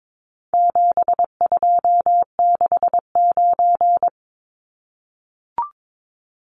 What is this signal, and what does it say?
Morse code, heard as a keyed tone, about 22 wpm, 700 Hz. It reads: 7269
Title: Numbers - 4-Digit 22wpm